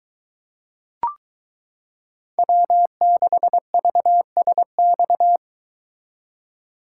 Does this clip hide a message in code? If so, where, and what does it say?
Yes: Morse code from 2.4 to 5.4 s, heard as a keyed tone, about 22 wpm, 700 Hz. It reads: W6VSX